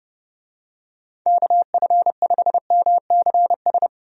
Morse code, heard as a keyed tone, about 32 wpm, 700 Hz. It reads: KF5MCS
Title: USA Call Signs - 6 Characters - Repeated 2x - 30wpm